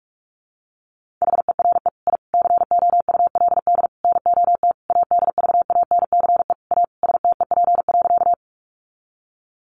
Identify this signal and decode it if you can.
Morse code, heard as a keyed tone, about 44 wpm, 700 Hz. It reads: HERE I COULD NOT ADVANCE A STEP.
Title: Sentences from Top 1400 Words 45wpm